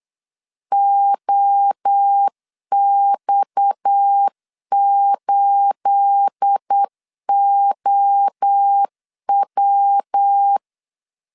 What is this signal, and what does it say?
Morse code, heard as a keyed tone, about 8 wpm, 800 Hz. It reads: OX8OW
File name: CW / DET / 1 9 1500C S C 8wpm 5653